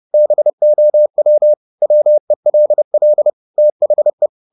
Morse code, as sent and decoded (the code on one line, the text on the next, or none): -... --- .-- / .-- . .-.. .-.. / - .... .
BOW WELL THE